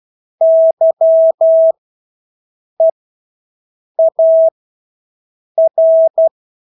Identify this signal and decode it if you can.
Morse code, heard as a keyed tone, about 12 wpm, spaced as for 6 wpm, 650 Hz. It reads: YEAR
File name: Engelske ord 12-6 wpm